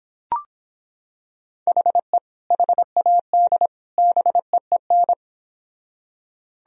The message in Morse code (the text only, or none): HE HAD BEEN